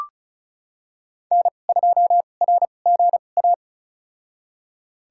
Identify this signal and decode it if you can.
Morse code, heard as a keyed tone, about 36 wpm, spaced as for 25 wpm, 700 Hz. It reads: N2RGA
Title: USA Call Signs - 5 Characters - 2x Character Spacing - 35wpm